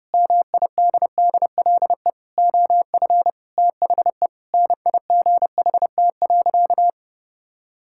Morse code, heard as a keyed tone, about 32 wpm, 700 Hz. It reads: MIDDLE OF THE NIGHT.
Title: Sentences from Top 1000 Words - Review Part 6 - 30wpm